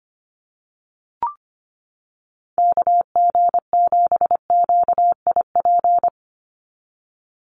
Morse code, heard as a keyed tone, about 25 wpm, 700 Hz. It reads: KG7QIP
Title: USA Call Signs 25wpm